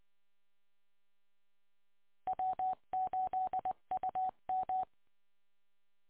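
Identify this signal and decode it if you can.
Morse code, heard as a keyed tone, about 23 wpm, 750 Hz. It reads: W8UM
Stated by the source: Morse code recorded from the W8UM 145.230 receiver audio archive.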